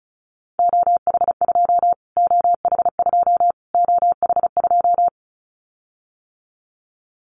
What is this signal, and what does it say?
Morse code, heard as a keyed tone, about 34 wpm, 700 Hz. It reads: OH2 OH2 OH2
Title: Callsign Prefixes for Top 25 DXCC - Repeated 3x - 35wpm